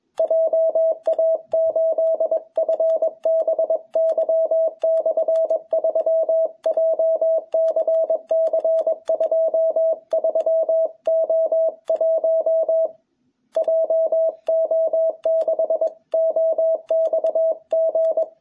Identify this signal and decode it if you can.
Morse code, heard as a keyed tone, about 21 wpm, 660 Hz. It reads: JA8FBY/3 JCC 23O1 JO6OXG
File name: jh8cbh / 2021cw